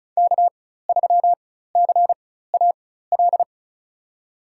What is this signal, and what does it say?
Morse code, heard as a keyed tone, about 36 wpm, spaced as for 16 wpm, 700 Hz. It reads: K3CAL
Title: USA Call Signs - 5 Characters - 4x Character Spacing - 35wpm